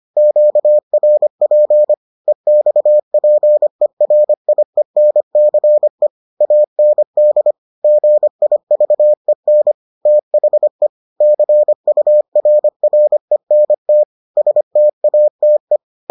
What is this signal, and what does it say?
Morse code, heard as a keyed tone, about 25 wpm, 600 Hz. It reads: QRP EXPERIENCE AND GIVEN THE CURRENT STATE